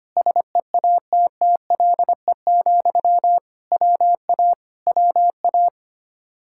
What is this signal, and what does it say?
Morse code, heard as a keyed tone, about 24 wpm, 700 Hz. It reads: SEATTLE, WA WA
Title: Speed-Racing Common QSO Phrases - 25wpm